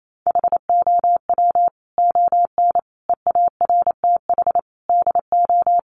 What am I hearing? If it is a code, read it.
Morse code, heard as a keyed tone, about 28 wpm, 700 Hz. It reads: HOW ON EARTH DO
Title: Sentences from Top 200 Words 28wpm